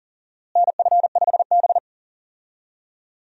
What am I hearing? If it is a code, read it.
Morse code, heard as a keyed tone, about 40 wpm, 700 Hz. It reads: NF5B
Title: USA Call Signs - 4 Characters - Repeated 2x - 40wpm